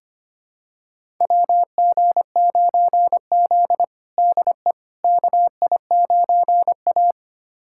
Morse code, heard as a keyed tone, about 25 wpm, 700 Hz. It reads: WG9Z DE KI9A
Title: USA Callsign DE Callsign - 4 Characters - 25wpm